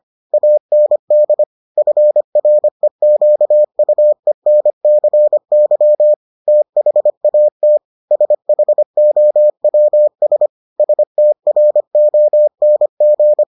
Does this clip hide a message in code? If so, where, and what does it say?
Yes: Morse code from 0.3 to 13.4 s, heard as a keyed tone, about 25 wpm, 600 Hz. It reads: AND FREQUENCY THAT SHOWS STRONG